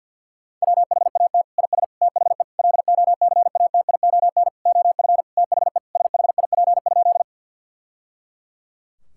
Morse code, heard as a keyed tone, about 50 wpm, 700 Hz. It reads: WHAT IS THE LOCATION OF THE SHIP?
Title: Sentences from Top 1000 Words - Review Part 6 - 50wpm